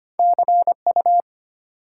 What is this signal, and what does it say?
Morse code, heard as a keyed tone, about 25 wpm, 700 Hz. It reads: CU
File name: Top 90 QSO Elements - 2x Character Spacing - 25wpm